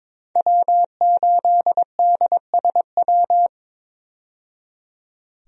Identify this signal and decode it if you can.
Morse code, heard as a keyed tone, about 21 wpm, 700 Hz. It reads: W8DSW
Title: USA Call Signs - 5 Characters - 22wpm